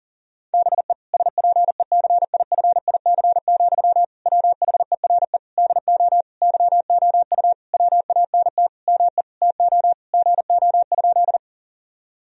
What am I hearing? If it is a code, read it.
Morse code, heard as a keyed tone, about 40 wpm, 700 Hz. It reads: BE SPECIFIC, WHERE DO YOU WANT ME TO GO?
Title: Sentences from Top 1000 Words - Review Part 3 - 40wpm